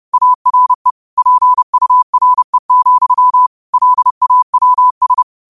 Morse code, heard as a keyed tone, about 30 wpm, 1000 Hz. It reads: ARE PURE, LAWS
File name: qotdfiles / 30wpm